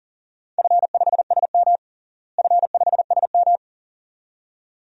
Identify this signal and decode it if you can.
Morse code, heard as a keyed tone, about 40 wpm, 700 Hz. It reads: F5SM F5SM
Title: DXCC-100 Call Signs - 4 Characters - Repeated 2x - 40wpm